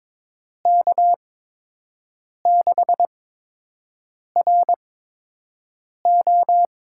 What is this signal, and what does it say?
Morse code, heard as a keyed tone, about 22 wpm, spaced as for 6 wpm, 700 Hz. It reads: K6RO
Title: USA Call Signs - 5 Characters - 8x Character Spacing - 22wpm